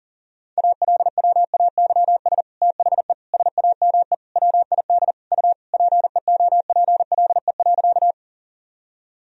Morse code, heard as a keyed tone, about 40 wpm, 700 Hz. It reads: ALWAYS THE SAME WID U PEOPLE.
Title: Sentences from Top 300 Words and Abbreviations 40wpm